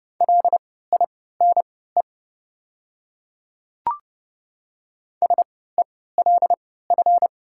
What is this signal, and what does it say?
Morse code, heard as a keyed tone, about 30 wpm, spaced as for 17 wpm, 690 Hz. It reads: LINE SELF
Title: 3x Character Spacing - Top 200 Words - 30wpm